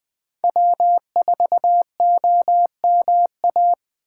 Morse code, heard as a keyed tone, about 20 wpm, 700 Hz. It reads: W4OMA
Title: USA Call Signs - 8x Character Spacing - 20wpm